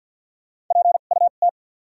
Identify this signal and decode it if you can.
Morse code, heard as a keyed tone, about 53 wpm, spaced as for 37 wpm, 700 Hz. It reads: PUT
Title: 2x Character Spacing - Top 200 Words - 50wpm